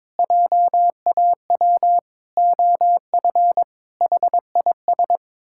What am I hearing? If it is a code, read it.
Morse code, heard as a keyed tone, about 22 wpm, 700 Hz. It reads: JAW OF HIS
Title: Pangram Challenge v2 - 22wpm